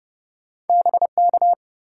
Morse code, heard as a keyed tone, about 30 wpm, 700 Hz. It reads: BK
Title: Top 180 QSO Elements 30wpm